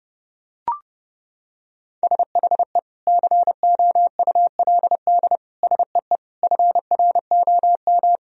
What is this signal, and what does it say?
Morse code, heard as a keyed tone, about 32 wpm, 700 Hz. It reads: SHE COULD SEE FROM
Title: Sentences from Top 1000 Words - Review Part 2 - 30wpm